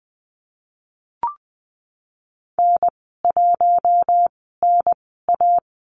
Morse code, heard as a keyed tone, about 20 wpm, spaced as for 15 wpm, 700 Hz. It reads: N1NA
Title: 140 Lesson 23g - Letter f - Call Signs 20wpm